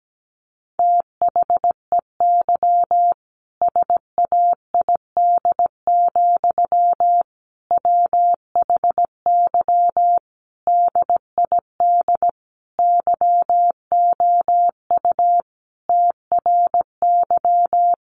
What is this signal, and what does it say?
Morse code, heard as a keyed tone, about 17 wpm, 700 Hz. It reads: THEY SAID, WHY DID YOU TRY
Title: Sentences from Top 1200 Words 17wpm